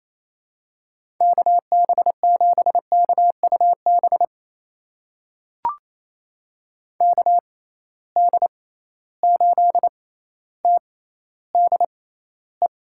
Morse code, heard as a keyed tone, about 28 wpm, spaced as for 14 wpm, 700 Hz. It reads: KB7KUB KD8TDE
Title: USA Call Signs - 6 Characters - 6x Character Spacing - 28wpm